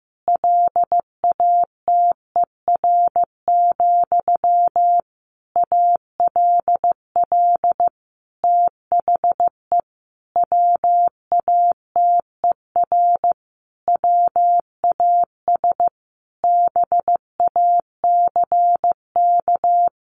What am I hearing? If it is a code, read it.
Morse code, heard as a keyed tone, about 15 wpm, 700 Hz. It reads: LATER, ALL THE WATER WAS BACK
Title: Sentences from Top 1100 Words 15wpm